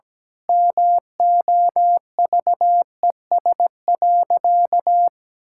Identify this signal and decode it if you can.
Morse code, heard as a keyed tone, about 17 wpm, 700 Hz. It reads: MOVES.
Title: Pangram Challenge v2 - 17wpm